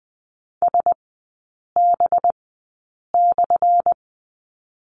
Morse code, heard as a keyed tone, about 20 wpm, spaced as for 8 wpm, 700 Hz. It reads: SB/
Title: 184 Lesson 30e - Letter b - Three Characters 20wpm